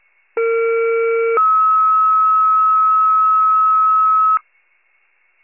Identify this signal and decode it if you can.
Two-tone page, heard as a one-second tone, then a three-second tone, about 460 then 1290 Hz. It is heard from 0.4 to 4.4 s.